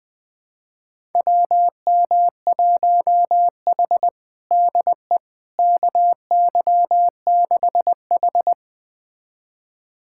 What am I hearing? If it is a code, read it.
Morse code, heard as a keyed tone, about 20 wpm, 700 Hz. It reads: WM1H DE KY6H
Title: USA Callsign DE Callsign - 4 Characters - 20wpm